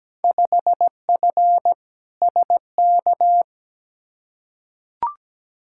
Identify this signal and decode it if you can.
Morse code, heard as a keyed tone, about 17 wpm, 700 Hz. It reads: 5F SK
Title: Field Day Exchanges Encore - 17wpm